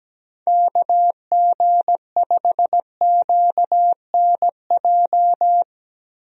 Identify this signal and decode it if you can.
Morse code, heard as a keyed tone, about 17 wpm, 700 Hz. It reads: KG5QNJ
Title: USA Call Signs - 6 Characters - 17wpm